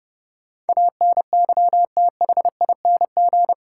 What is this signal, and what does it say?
Morse code, heard as a keyed tone, about 30 wpm, 700 Hz. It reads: ANYTHING
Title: Sets of 4 Words - New Word Drill - 30wpm